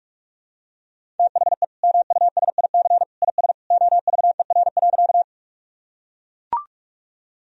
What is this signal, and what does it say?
Morse code, heard as a keyed tone, about 43 wpm, 700 Hz. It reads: THE MUSIC IS OVER.
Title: Sentences from Top 1000 Words - Review Part 1 - 45wpm